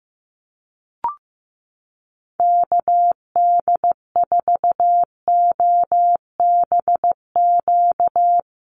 Morse code, heard as a keyed tone, about 15 wpm, 700 Hz. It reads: KD4OBQ
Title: USA Call Signs 15wpm